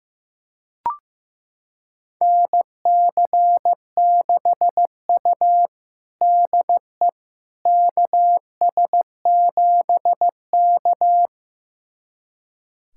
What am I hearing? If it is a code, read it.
Morse code, heard as a keyed tone, about 15 wpm, 700 Hz. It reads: NC6U DE KS7K
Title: USA Callsign DE Callsign - 4 Characters - 15wpm